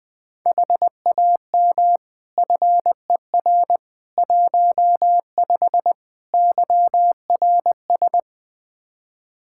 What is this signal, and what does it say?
Morse code, heard as a keyed tone, about 20 wpm, 700 Hz. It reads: HAM FER 15 YRS
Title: Practice QSO - Exchange 3 - More Info - 20wpm